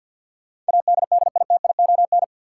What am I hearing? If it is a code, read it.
Morse code, heard as a keyed tone, about 50 wpm, 700 Hz. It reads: ADDITION
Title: Sentences from Top 1000 Words - Review Part 6 - New Word Drill - 50wpm